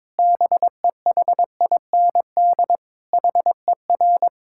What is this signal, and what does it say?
Morse code, heard as a keyed tone, about 22 wpm, 700 Hz. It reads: BEHIND HER
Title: Sentences from Top 1000 Words - Review Part 6 - 22wpm